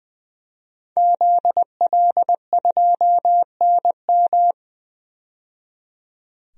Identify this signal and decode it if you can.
Morse code, heard as a keyed tone, about 20 wpm, 700 Hz. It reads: ZL2NM
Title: DXCC-100 Call Signs - 5 Characters - 20wpm